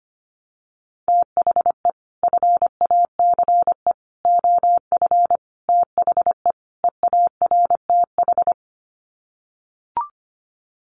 Morse code, heard as a keyed tone, about 25 wpm, 700 Hz. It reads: THE FACE OF THE EARTH
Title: Sets of 5 Words Encore 25wpm